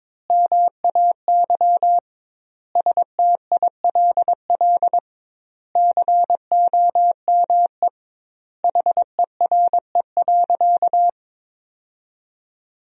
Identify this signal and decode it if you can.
Morse code, heard as a keyed tone, about 22 wpm, 700 Hz. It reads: MAY STILL COME HERE.
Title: 2x Word-Spacing - Sentences from Top 300 Words - 22wpm